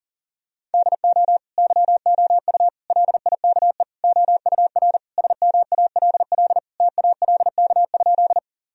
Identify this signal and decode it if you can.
Morse code, heard as a keyed tone, about 40 wpm, 700 Hz. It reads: DO YOU LIKE OUR SMALL TALK?
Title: Sentences from Top 1000 Words - Review Part 1 - 40wpm